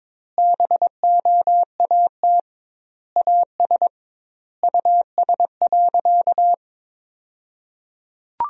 Morse code, heard as a keyed tone, about 22 wpm, 700 Hz. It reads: BOAT AS US.
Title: 2x Word-Spacing - Sentences from Top 500 Words - 22wpm